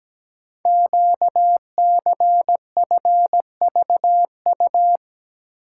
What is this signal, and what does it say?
Morse code, heard as a keyed tone, about 17 wpm, 700 Hz. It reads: QCFVU
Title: Code Groups - 5 Letters - 17wpm